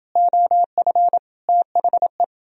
Morse code, heard as a keyed tone, about 29 wpm, 700 Hz. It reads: OF THE